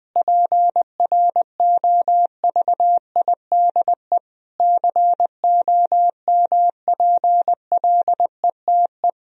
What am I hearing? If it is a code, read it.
Morse code, heard as a keyed tone, about 20 wpm, 700 Hz. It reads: PROVIDE COMPLETE
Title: Sentences from Top 1000 Words - Review Part 4 - 20wpm